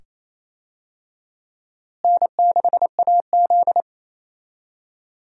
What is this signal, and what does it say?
Morse code, heard as a keyed tone, about 28 wpm, 700 Hz. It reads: N6AZ
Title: USA Call Signs - 4 Characters - Repeated 2x - 28wpm